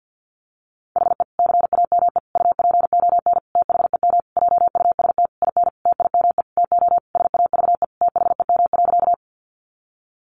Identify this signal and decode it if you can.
Morse code, heard as a keyed tone, about 50 wpm, 700 Hz. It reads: HE CAME UPON THEM JUST IN TIME TO SAVE THEM.
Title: Sentences from Top 1100 Words 50wpm